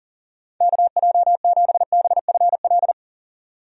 Morse code, heard as a keyed tone, about 40 wpm, 700 Hz. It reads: KJ7BFL
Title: USA Call Signs - 6 Characters - 40wpm